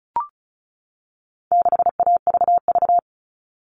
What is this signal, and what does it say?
Morse code, heard as a keyed tone, about 34 wpm, 700 Hz. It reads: 6AVV
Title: All Cut Numbers - 4-Digit 35wpm